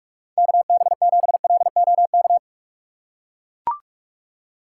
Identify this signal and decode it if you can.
Morse code, heard as a keyed tone, about 43 wpm, 700 Hz. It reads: KB7LOK
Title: USA Call Signs - 6 Characters - 6x Character Spacing - 45wpm